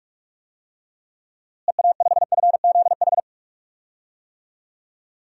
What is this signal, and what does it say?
Morse code, heard as a keyed tone, about 43 wpm, 700 Hz. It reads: EA5FZH
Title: DXCC-100 Call Signs - 6 Characters - Repeated 2x - 45wpm